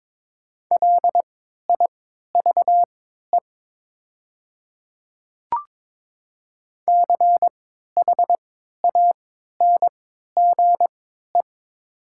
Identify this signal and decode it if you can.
Morse code, heard as a keyed tone, about 22 wpm, spaced as for 12 wpm, 700 Hz. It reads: LIVE CHANGE
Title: 3x Character Spacing - Top 200 Words - 22wpm